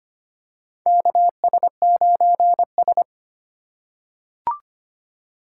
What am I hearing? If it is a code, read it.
Morse code, heard as a keyed tone, about 25 wpm, 700 Hz. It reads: KS9S
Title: USA Call Signs - 4 Characters - 4x Character Spacing - 25wpm